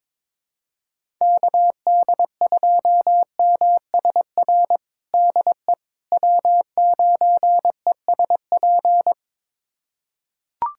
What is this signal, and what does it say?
Morse code, heard as a keyed tone, about 22 wpm, 700 Hz. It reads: KD2MSR DE W9ESP
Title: USA Callsign DE Callsign 22wpm